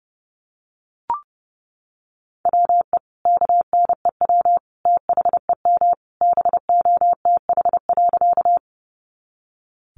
Morse code, heard as a keyed tone, about 30 wpm, 700 Hz. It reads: WE KNEW THEM BOTH.
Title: Sentences from Top 400 Words 30wpm